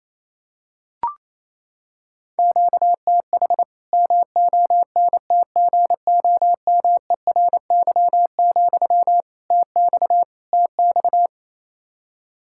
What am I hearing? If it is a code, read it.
Morse code, heard as a keyed tone, about 28 wpm, 700 Hz. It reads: QTH MONTGOMERY, TX TX
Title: Practice QSO - Exchange 1 - Beginning - 28wpm